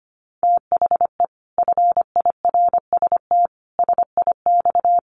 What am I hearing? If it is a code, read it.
Morse code, heard as a keyed tone, about 25 wpm, 700 Hz. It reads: THE FIRST SIX
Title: Sets of 5 Words Encore2 25wpm